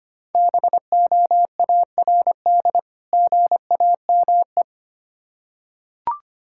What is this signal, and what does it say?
Morse code, heard as a keyed tone, about 25 wpm, 700 Hz. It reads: BOARD GAME
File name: practice / 25wpm / Difficult Words with Hints - 25wpm